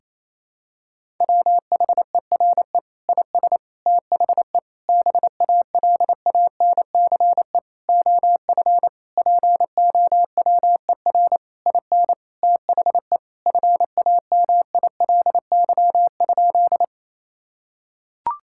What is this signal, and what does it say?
Morse code, heard as a keyed tone, about 28 wpm, 700 Hz. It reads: WHERE IS THE BALANCE OF POWER IN THE FAMILY?
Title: Sentences from Top 1000 Words - Review Part 6 - 28wpm